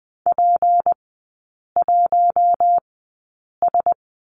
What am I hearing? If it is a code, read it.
Morse code, heard as a keyed tone, about 20 wpm, spaced as for 8 wpm, 690 Hz. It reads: P1S
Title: 198 Lesson 32e - Letter k - Three Characters 20wpm